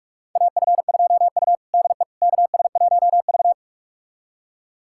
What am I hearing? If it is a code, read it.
Morse code, heard as a keyed tone, about 41 wpm, 700 Hz. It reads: AF2U DE KS1V
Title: USA Callsign DE Callsign - 4 Characters - 45wpm